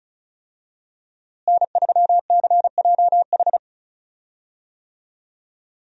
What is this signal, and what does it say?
Morse code, heard as a keyed tone, about 34 wpm, 700 Hz. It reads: N3CJH
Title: USA Call Signs - 5 Characters - Repeated 2x - 35wpm